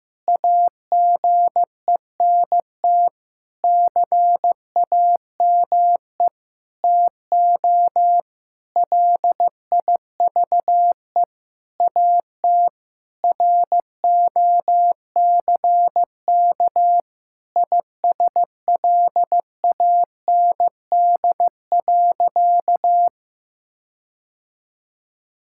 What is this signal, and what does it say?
Morse code, heard as a keyed tone, about 16 wpm, 700 Hz. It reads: AGENT CAME TO LIVE AT ROCK ISLAND.
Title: Sentences from Top 1600 Words 15wpm